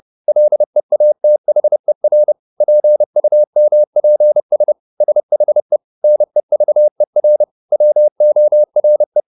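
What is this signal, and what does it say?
Morse code, heard as a keyed tone, about 30 wpm, 600 Hz. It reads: LEATHER PUMPS SHE NEVER WORE
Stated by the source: Morse code at 30 wpm, the novel Never by Ken Follett in English, from CW med Gnister.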